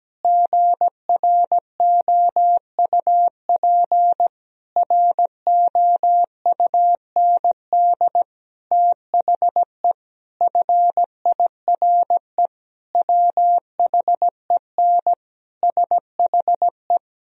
Morse code, heard as a keyed tone, about 17 wpm, 700 Hz. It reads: GROUP ROUND THE FIRE WHEN SHE
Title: Sentences from Top 1000 Words - Review Part 1 - 17wpm